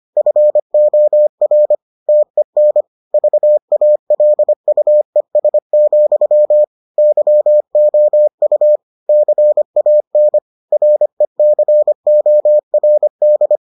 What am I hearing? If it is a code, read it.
Morse code, heard as a keyed tone, about 25 wpm, 600 Hz. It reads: FOR TEN VALUES, YOU CAN RECORD